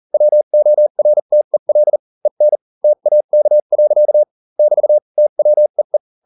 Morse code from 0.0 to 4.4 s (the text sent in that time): WORTEL EN TAK.